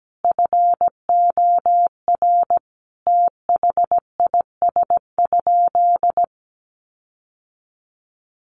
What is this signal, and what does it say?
Morse code, heard as a keyed tone, about 17 wpm, 700 Hz. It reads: FOR THIS?
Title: Sentences from Top 1400 Words 17wpm